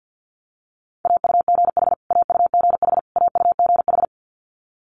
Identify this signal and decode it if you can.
Morse code, heard as a keyed tone, about 50 wpm, 700 Hz. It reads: AUGH AUGH AUGH
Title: Multi-letter Phonemes Repeated 3x 50wpm